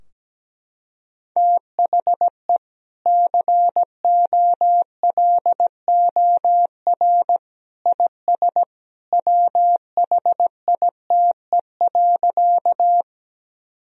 Morse code, heard as a keyed tone, about 17 wpm, 700 Hz. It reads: THE COLOR IS WHITE.
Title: Sentences from Top 1000 Words - Review Part 2 - 17wpm